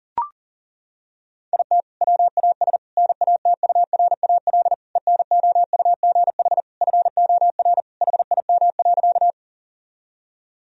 Morse code, heard as a keyed tone, about 40 wpm, 700 Hz. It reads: IT WAS NATURAL ENOUGH FOR HIM.
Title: Sentences from Top 1000 Words - Review Part 4 - 40wpm